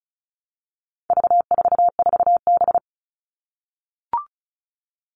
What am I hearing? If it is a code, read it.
Morse code, heard as a keyed tone, about 34 wpm, 700 Hz. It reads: V44B
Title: All Cut Numbers - 4-Digit 35wpm